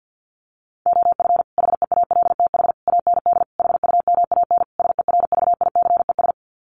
Morse code, heard as a keyed tone, about 50 wpm, 700 Hz. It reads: OF HEALTH AND HUMAN SERVICES
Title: Sets of 5 Words 50wpm